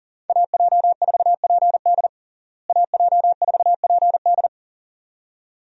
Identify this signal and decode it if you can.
Morse code, heard as a keyed tone, about 40 wpm, 700 Hz. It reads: AJ4PD AJ4PD
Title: USA Call Signs - 5 Characters - Repeated 2x - 40wpm